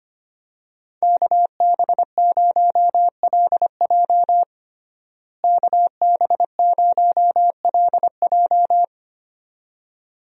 Morse code, heard as a keyed tone, about 25 wpm, 700 Hz. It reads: KB0LJ KB0LJ
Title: USA Call Signs - 5 Characters - Repeated 2x - 25wpm